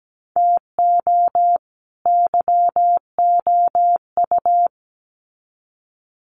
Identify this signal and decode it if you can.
Morse code, heard as a keyed tone, about 17 wpm, 700 Hz. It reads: TO YOU
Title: Sets of 2 Words 17wpm